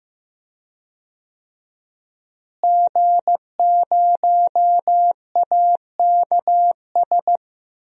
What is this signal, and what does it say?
Morse code, heard as a keyed tone, about 15 wpm, 700 Hz. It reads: G0AKS
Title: DXCC-100 Call Signs - 5 Characters - Repeated 2x - 15wpm